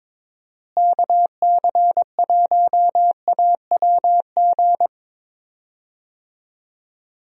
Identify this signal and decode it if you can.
Morse code, heard as a keyed tone, about 22 wpm, 700 Hz. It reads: KC1AWG
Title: USA Call Signs - 6 Characters - Repeated 3x - 22wpm